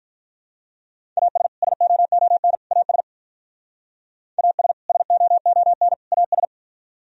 Morse code, heard as a keyed tone, about 44 wpm, 700 Hz. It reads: AS SOON AS AS SOON AS